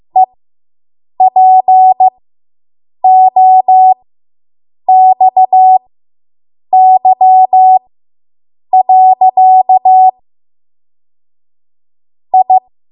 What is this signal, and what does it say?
Morse code, heard as a keyed tone, about 15 wpm, spaced as for 7 wpm, 750 Hz. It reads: EPOXY. I